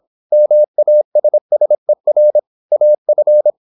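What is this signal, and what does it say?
Morse code, heard as a keyed tone, about 26 wpm, 600 Hz. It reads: MASSER AF
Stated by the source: High-speed (QRQ) Morse at 26 wpm.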